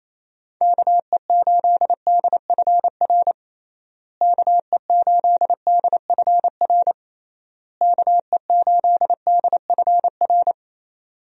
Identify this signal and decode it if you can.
Morse code, heard as a keyed tone, about 28 wpm, 700 Hz. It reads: KE8DFR KE8DFR KE8DFR
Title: USA Call Signs - 6 Characters - Repeated 3x - 28wpm